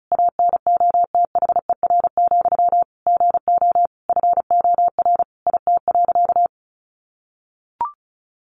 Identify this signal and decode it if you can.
Morse code, heard as a keyed tone, about 34 wpm, 700 Hz. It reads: ANOTHER, GO FOR IT.
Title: Sentences from Top 1100 Words 35wpm